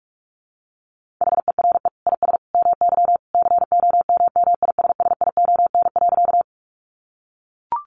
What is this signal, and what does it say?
Morse code, heard as a keyed tone, about 44 wpm, 700 Hz. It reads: HERE IS MY COMMISSION.
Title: Sentences from Top 1400 Words 45wpm